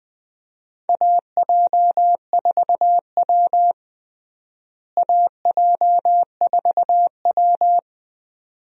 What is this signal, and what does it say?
Morse code, heard as a keyed tone, about 20 wpm, 700 Hz. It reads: AJ4W AJ4W
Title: USA Call Signs - 4 Characters - Repeated 2x - 20wpm